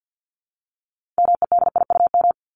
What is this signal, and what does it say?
Morse code, heard as a keyed tone, about 50 wpm, 700 Hz. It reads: MEDIUM
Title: Top 751-1000 Words 50wpm